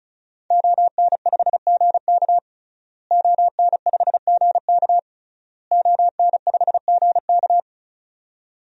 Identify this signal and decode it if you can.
Morse code, heard as a keyed tone, about 34 wpm, 700 Hz. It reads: ON5GK ON5GK ON5GK
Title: DXCC-100 Call Signs - 5 Characters - Repeated 3x - 35wpm